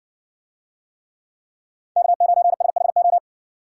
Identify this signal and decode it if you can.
Morse code, heard as a keyed tone, about 55 wpm, 700 Hz. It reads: K9SHO